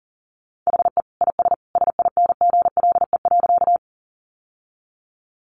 Morse code, heard as a keyed tone, about 40 wpm, 700 Hz. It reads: HE IS SINGLE.
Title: Sentences from Top 700 Words 40wpm